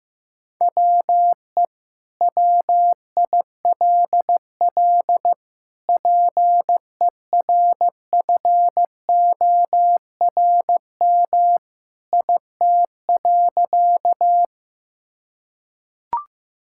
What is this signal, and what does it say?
Morse code, heard as a keyed tone, about 15 wpm, 700 Hz. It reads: WE WILL PERFORM IT.
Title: Sentences from Top 1000 Words - Review Part 5 - 15wpm